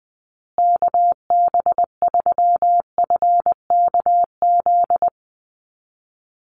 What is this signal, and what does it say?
Morse code, heard as a keyed tone, about 20 wpm, 700 Hz. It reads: KB3FKZ
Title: USA Call Signs Repeated 3x 20wpm